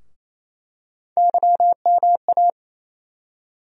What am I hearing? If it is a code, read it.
Morse code, heard as a keyed tone, about 28 wpm, 700 Hz. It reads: YMA
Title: USA Call Sign Postfixes - 3 Letters 28wpm